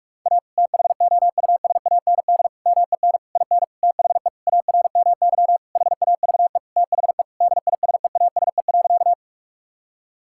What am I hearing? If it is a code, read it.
Morse code, heard as a keyed tone, about 46 wpm, 700 Hz. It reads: A THOUSAND MEN IN THE ARMY HAVE THE DISEASE.
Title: Sentences from Top 1000 Words - Review Part 6 - 45wpm